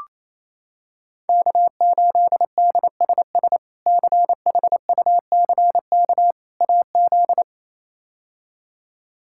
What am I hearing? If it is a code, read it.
Morse code, heard as a keyed tone, about 28 wpm, 700 Hz. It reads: K8DSS CHUCK AZ
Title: K1USN SST QSO Practice - 28wpm